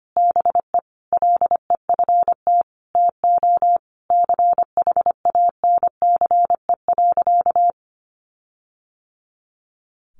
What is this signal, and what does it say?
Morse code, heard as a keyed tone, about 25 wpm, 700 Hz. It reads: BE LEFT TO CHANCE.
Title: Sentences from Top 1400 Words 25wpm